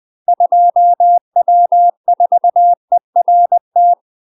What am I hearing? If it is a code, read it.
Morse code, heard as a keyed tone, about 20 wpm, 700 Hz. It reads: 2W4ERT
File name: UK MASTER QSOs TRAINING / 20WPM / UK MASTER QSOs 0063